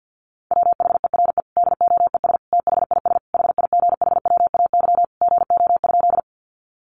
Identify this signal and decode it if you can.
Morse code, heard as a keyed tone, about 50 wpm, 700 Hz. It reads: WHERE DOES THIS HIGHWAY GO?
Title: Sentences from Top 1400 Words 50wpm